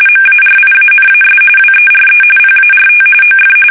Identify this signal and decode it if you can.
RTTY, about 45.45 baud. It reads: MARIANAS WERE PART OF
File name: fsk850 / battleshipnj